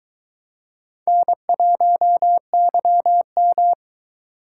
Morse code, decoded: N1YM

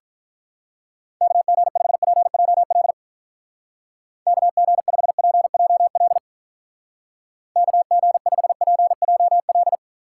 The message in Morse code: KG5PJL KG5PJL KG5PJL

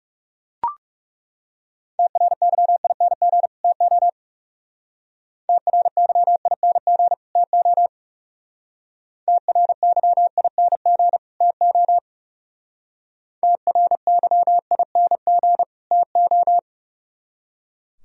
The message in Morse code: TRYING TO TRYING TO TRYING TO TRYING TO

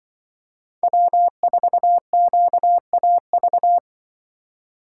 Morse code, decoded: W4QAV

